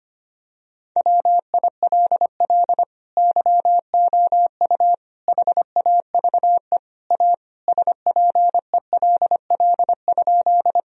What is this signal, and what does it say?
Morse code, heard as a keyed tone, about 25 wpm, 700 Hz. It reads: WILL YOU HAVE A SPELL?
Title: Sentences from Top 1000 Words - Review Part 1 - 25wpm